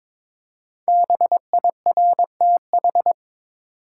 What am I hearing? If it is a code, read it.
Morse code, heard as a keyed tone, about 22 wpm, 700 Hz. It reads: BIRTH
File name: 5 Letter Words - First 3 Letters Given - 22wpm